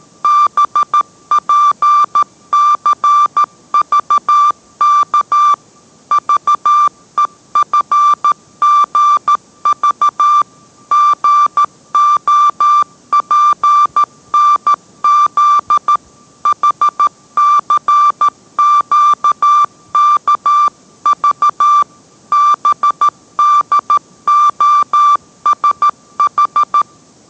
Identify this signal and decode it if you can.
Morse code, heard as a keyed tone, about 13 wpm, 1180 Hz. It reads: BPCVK VEFGV GOPNZ HCQKV BDOSH